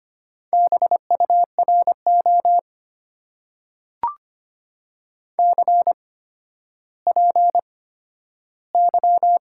Morse code, heard as a keyed tone, about 25 wpm, spaced as for 12 wpm, 690 Hz. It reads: BURO CPY
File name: Top 90 QSO Elements - 8x Character Spacing - 25wpm